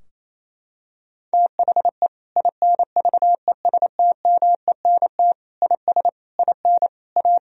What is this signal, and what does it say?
Morse code, heard as a keyed tone, about 28 wpm, 700 Hz. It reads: THE INVESTMENT IS IN A
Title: Sentences from Top 1000 Words - Review Part 4 - 28wpm